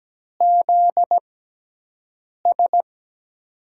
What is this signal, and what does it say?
Morse code, heard as a keyed tone, about 17 wpm, spaced as for 6 wpm, 700 Hz. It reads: ZS
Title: USA Call Signs - 5 Characters - 6x Character Spacing - 17wpm